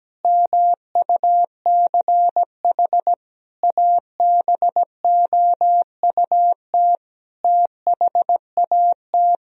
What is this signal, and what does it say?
Morse code, heard as a keyed tone, about 17 wpm, 700 Hz. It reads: MUCH ABOUT THAT
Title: Sentences from Top 1000 Words - Review Part 4 - 17wpm